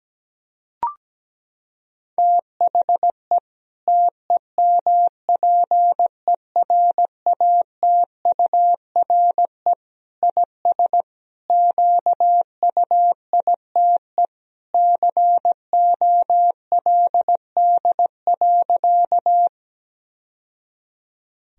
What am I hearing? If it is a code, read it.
Morse code, heard as a keyed tone, about 17 wpm, 700 Hz. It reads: THE TEMPERATURE IS QUITE COLD.
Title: Sentences from Top 1000 Words - Review Part 4 - 17wpm